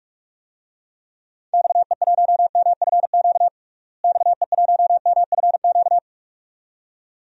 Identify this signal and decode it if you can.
Morse code, heard as a keyed tone, about 41 wpm, 700 Hz. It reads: XE1MFQ XE1MFQ
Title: DXCC-100 Call Signs - 6 Characters - Repeated 2x - 45wpm